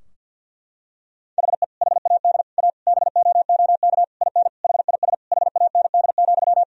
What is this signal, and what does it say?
Morse code, heard as a keyed tone, about 50 wpm, 700 Hz. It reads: HE HAD A BOOK IN HIS HAND,
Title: Sentences from Top 1000 Words - Review Part 3 - 50wpm